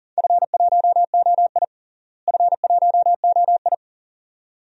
Morse code, heard as a keyed tone, about 40 wpm, 700 Hz. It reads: F1OI F1OI
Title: DXCC-100 Call Signs - 4 Characters - Repeated 2x - 40wpm